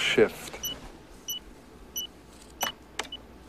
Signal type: none